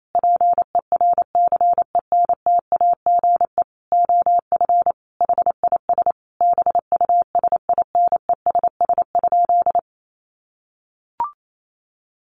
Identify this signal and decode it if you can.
Morse code, heard as a keyed tone, about 28 wpm, 700 Hz. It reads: PERCENTAGE OF HIS BUSINESS?
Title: Sentences from Top 1300 Words 28wpm